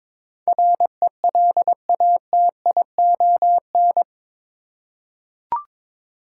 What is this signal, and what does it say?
Morse code, heard as a keyed tone, about 22 wpm, 700 Hz. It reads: RELATION
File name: practice / 22wpm / Eight Letter Words - 22wpm